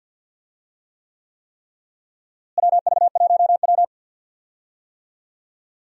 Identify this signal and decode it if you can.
Morse code, heard as a keyed tone, about 51 wpm, 700 Hz. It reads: WV1W
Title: USA Call Signs - 4 Characters - Repeated 3x - 50wpm